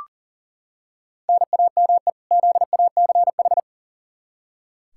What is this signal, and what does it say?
Morse code, heard as a keyed tone, about 40 wpm, 700 Hz. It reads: NAME ZACH
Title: K1USN SST QSO Practice 40wpm